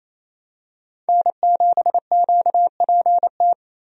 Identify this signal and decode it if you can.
Morse code, heard as a keyed tone, about 27 wpm, 700 Hz. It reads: N7QPT
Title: USA Call Signs - 5 Characters - Repeated 3x - 28wpm